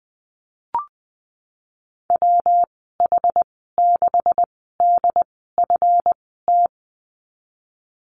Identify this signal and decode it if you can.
Morse code, heard as a keyed tone, about 20 wpm, spaced as for 15 wpm, 700 Hz. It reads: WH6DFT
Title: 161 Lesson 26g - Letter g - Call Signs 20wpm